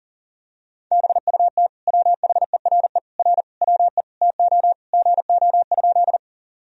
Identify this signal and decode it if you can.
Morse code, heard as a keyed tone, about 37 wpm, 700 Hz. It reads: BUT WHERE R WE TO GO?
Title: Sentences from Top 200 Words and Abbreviations 40wpm